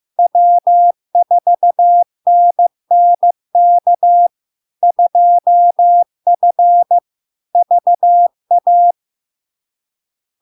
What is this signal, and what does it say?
Morse code, heard as a keyed tone, about 15 wpm, 700 Hz. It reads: W4NNK 2F VA
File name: LARK-FD-2022-Log-in-CW-15wpm-700-Hz